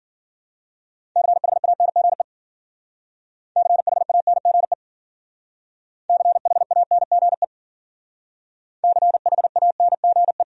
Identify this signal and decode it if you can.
Morse code, heard as a keyed tone, about 51 wpm, 700 Hz. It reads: CHANGE CHANGE CHANGE CHANGE